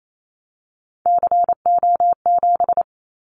Morse code, heard as a keyed tone, about 28 wpm, 700 Hz. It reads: CO7
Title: Callsign Prefixes for Top 100 DXCC - 28wpm